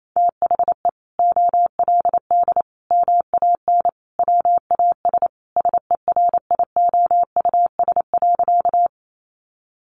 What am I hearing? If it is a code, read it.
Morse code, heard as a keyed tone, about 28 wpm, 700 Hz. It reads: THE OLD MAN WAS SERIOUS.
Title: Sentences from Top 1100 Words 28wpm